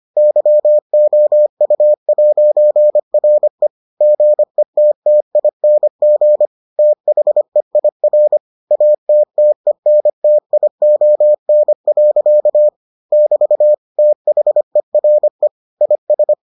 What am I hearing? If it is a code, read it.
Morse code, heard as a keyed tone, about 25 wpm, 600 Hz. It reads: YOU'RE GETTING THEIR ATTENTION. <BT> THERE IS